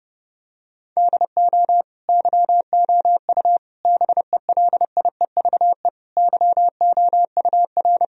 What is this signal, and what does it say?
Morse code, heard as a keyed tone, about 30 wpm, 700 Hz. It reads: DO YOU BELIEVE YOUR
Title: Sentences from Top 1000 Words - Review Part 3 - 30wpm